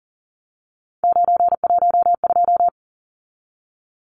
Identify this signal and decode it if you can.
Morse code, heard as a keyed tone, about 40 wpm, 700 Hz. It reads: 912
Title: Numbers - 3-Digit 40wpm